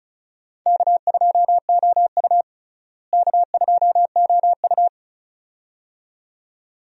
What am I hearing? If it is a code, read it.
Morse code, heard as a keyed tone, about 34 wpm, 700 Hz. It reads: K2OU K2OU
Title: USA Call Signs - 4 Characters - Repeated 2x - 35wpm